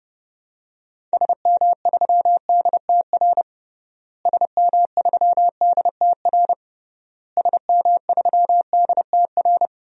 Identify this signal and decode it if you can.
Morse code, heard as a keyed tone, about 30 wpm, 700 Hz. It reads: SM3DTR SM3DTR SM3DTR
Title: DXCC-100 Call Signs - 6 Characters - Repeated 3x - 30wpm